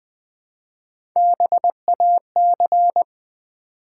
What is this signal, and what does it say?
Morse code, heard as a keyed tone, about 20 wpm, 700 Hz. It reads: BAC
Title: Build A Word - 1 Letter At A Time - 3 to 5-Letter Words - 20wpm